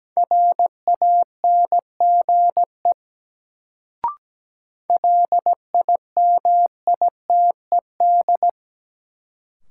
Morse code, heard as a keyed tone, about 17 wpm, 700 Hz. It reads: RANGE LIMITED
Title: Sentences from Top 900 Words - New Word Drill - 17wpm